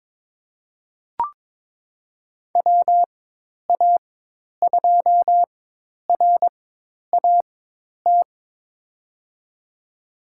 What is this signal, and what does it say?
Morse code, heard as a keyed tone, about 22 wpm, spaced as for 10 wpm, 700 Hz. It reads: WA2RAT
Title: USA Call Signs - 6 Characters - 4x Character Spacing - 22wpm